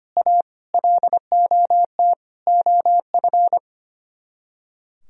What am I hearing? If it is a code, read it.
Morse code, heard as a keyed tone, about 30 wpm, 700 Hz. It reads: A LOT OF
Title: Speed-Racing - Sets of 4 Words 25wpm